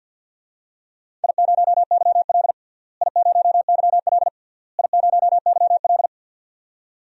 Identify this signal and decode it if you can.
Morse code, heard as a keyed tone, about 50 wpm, 700 Hz. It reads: I0YL I0YL I0YL
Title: DXCC-100 Call Signs - 4 Characters - Repeated 3x - 50wpm